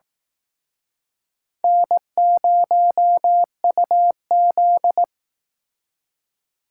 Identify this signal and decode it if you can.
Morse code, heard as a keyed tone, about 19 wpm, 700 Hz. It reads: N0UZ